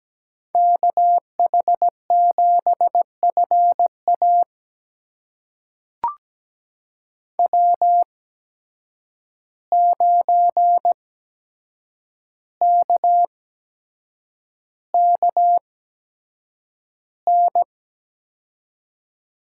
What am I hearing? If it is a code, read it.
Morse code, heard as a keyed tone, about 17 wpm, spaced as for 7 wpm, 700 Hz. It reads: KH7FA W9KKN
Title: USA Call Signs - 5 Characters - 8x Character Spacing - 17wpm